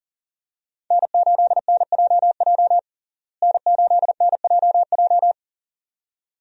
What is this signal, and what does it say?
Morse code, heard as a keyed tone, about 37 wpm, 700 Hz. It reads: N8NJJ N8NJJ
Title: USA Call Signs - 5 Characters - Repeated 2x - 40wpm